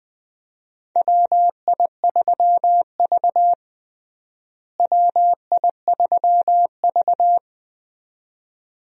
Morse code, heard as a keyed tone, about 20 wpm, 700 Hz. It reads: WI3V WI3V
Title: USA Call Signs - 4 Characters - Repeated 2x - 20wpm